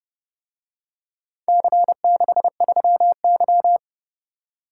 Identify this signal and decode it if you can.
Morse code, heard as a keyed tone, about 28 wpm, 700 Hz. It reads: C63Y